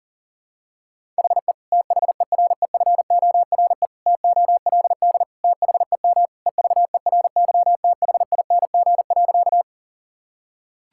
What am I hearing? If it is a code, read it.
Morse code, heard as a keyed tone, about 40 wpm, 700 Hz. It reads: HE THEREFORE TOLD THEM EVERYTHING.
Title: Sentences from Top 1000 Words - Review Part 5 - 40wpm